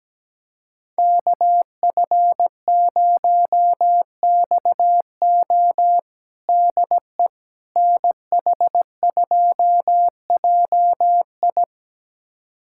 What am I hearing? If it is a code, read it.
Morse code, heard as a keyed tone, about 17 wpm, 700 Hz. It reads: KF0XO DE NH2JI
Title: USA Callsign DE Callsign - 5 Characters - 17wpm